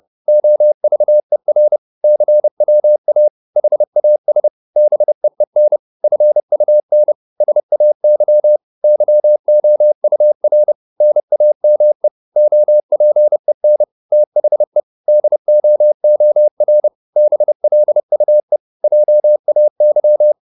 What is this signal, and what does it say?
Morse code, heard as a keyed tone, about 30 wpm, 600 Hz. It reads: OVER CWA HAS BEEN FUN SAY YOUR NAME OPEN THE DOOR BLUE JAY